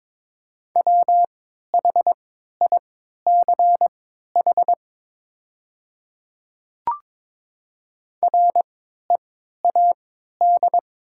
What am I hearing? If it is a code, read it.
Morse code, heard as a keyed tone, about 22 wpm, spaced as for 13 wpm, 700 Hz. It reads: WHICH READ
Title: 3x Character Spacing - Top 500 Words - 22wpm